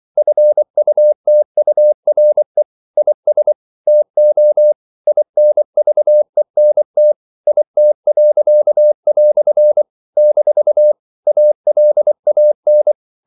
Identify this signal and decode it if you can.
Morse code, heard as a keyed tone, about 24 wpm, 600 Hz. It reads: FUTURE IS TO INVENT IT." - ALAN
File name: CW / Practice / EnglishQuotations0424